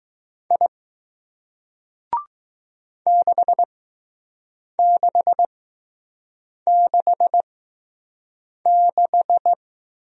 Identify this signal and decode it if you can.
Morse code, heard as a keyed tone, about 20 wpm, spaced as for 6 wpm, 700 Hz. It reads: I 6666